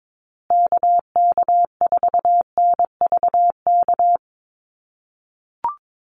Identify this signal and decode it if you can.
Morse code, heard as a keyed tone, about 22 wpm, 700 Hz. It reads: KK4NVK
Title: USA Call Signs Repeated 2x 22wpm